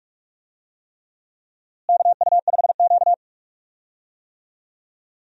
Morse code, heard as a keyed tone, about 46 wpm, 700 Hz. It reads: KU5Q
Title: USA Call Signs - 4 Characters - Repeated 3x - 45wpm